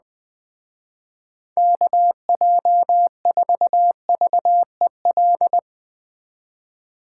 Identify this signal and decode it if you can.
Morse code, heard as a keyed tone, about 20 wpm, 700 Hz. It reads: KJ4VEL